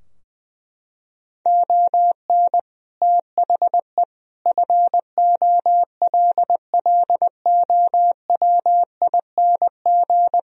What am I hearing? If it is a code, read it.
Morse code, heard as a keyed tone, about 20 wpm, 700 Hz. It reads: ON THE FOLLOWING
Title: Sentences from Top 1000 Words - Review Part 4 - 20wpm